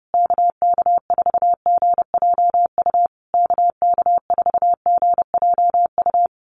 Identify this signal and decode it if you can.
Morse code, heard as a keyed tone, about 30 wpm, 700 Hz. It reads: KK4GJU KK4GJU
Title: USA Call Signs Repeated 3x 30wpm